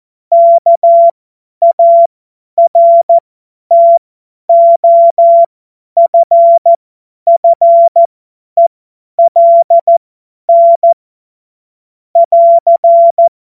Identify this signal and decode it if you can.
Morse code, heard as a keyed tone, about 14 wpm, spaced as for 10 wpm, 680 Hz. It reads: KARTOFFELN <AR>
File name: Morsekurs / Witze / Witz 04. Mai 2021, 50 F70